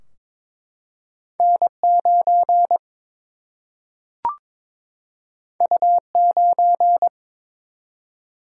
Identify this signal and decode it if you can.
Morse code, heard as a keyed tone, about 21 wpm, 700 Hz. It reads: N9 U9
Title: Single-Letter Single-Number 22wpm